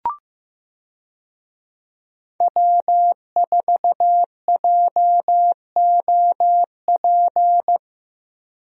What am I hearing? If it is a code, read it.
Morse code, heard as a keyed tone, about 15 wpm, 700 Hz. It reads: W4JOP
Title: USA Call Signs - 5 Characters - Repeated 3x - 15wpm